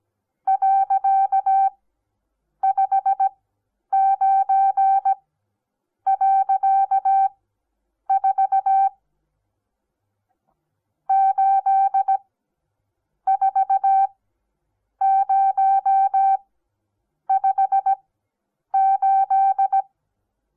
Morse code, heard as a keyed tone, about 17 wpm, spaced as for 8 wpm, 780 Hz. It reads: .59.4 84058
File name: Onlinelehrgang / morsen / DJ4UF Morsen08